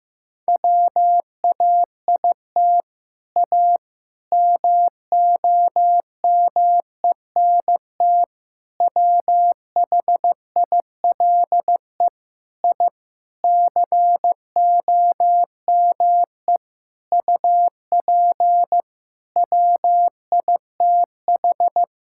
Morse code, heard as a keyed tone, about 15 wpm, 700 Hz. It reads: WAIT A MOMENT WHILE I COME UP WITH
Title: Sentences from Top 1000 Words - Review Part 5 - 15wpm